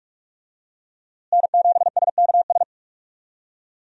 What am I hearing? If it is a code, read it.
Morse code, heard as a keyed tone, about 46 wpm, 700 Hz. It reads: N7SKS
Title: USA Call Signs - 5 Characters - Repeated 2x - 45wpm